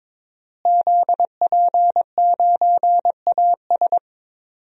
Morse code, heard as a keyed tone, about 22 wpm, 700 Hz. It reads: ZP9AS
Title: DXCC-100 Call Signs - 5 Characters - Repeated 3x - 22wpm